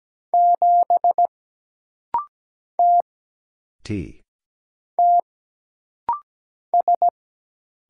Morse code, heard as a keyed tone, about 17 wpm, 700 Hz. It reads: 7 T T S
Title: Single Letter-Number Alt - Rapid-Fire 17wpm